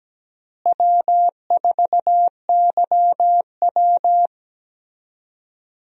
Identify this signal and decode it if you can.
Morse code, heard as a keyed tone, about 17 wpm, 700 Hz. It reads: W4YW
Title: USA Call Signs - 4 Characters - Repeated 2x - 17wpm